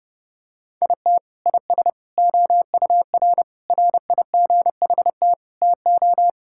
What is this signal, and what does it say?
Morse code, heard as a keyed tone, about 30 wpm, 700 Hz. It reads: IT IS OUR RIGHT TO
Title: Sentences from Top 1000 Words - Review Part 5 - 30wpm